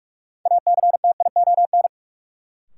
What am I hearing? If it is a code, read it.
Morse code, heard as a keyed tone, about 46 wpm, 700 Hz. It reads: ACTION
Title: Sentences from Top 700 Words - New Word Drill - 45wpm